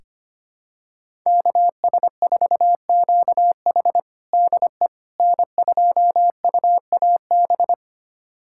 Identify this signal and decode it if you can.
Morse code, heard as a keyed tone, about 25 wpm, 700 Hz. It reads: KS4QH DE N2UAB
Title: USA Callsign DE Callsign - 5 Characters - 25wpm